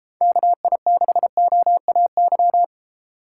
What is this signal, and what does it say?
Morse code, heard as a keyed tone, about 32 wpm, 700 Hz. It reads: KI6OAY